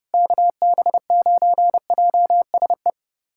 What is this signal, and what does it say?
Morse code, heard as a keyed tone, about 30 wpm, 700 Hz. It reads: KB9JSE
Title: USA Callsign DE Callsign 30wpm